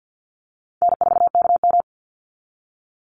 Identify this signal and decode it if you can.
Morse code, heard as a keyed tone, about 48 wpm, 700 Hz. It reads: N4KM
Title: Call Signs Repeated 2x - 50wpm